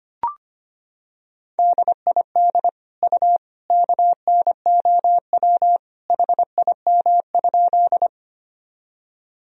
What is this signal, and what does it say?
Morse code, heard as a keyed tone, about 25 wpm, 700 Hz. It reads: DID U KNOW HIM?
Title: Sentences from Top 100 Words and Abbreviations 25wpm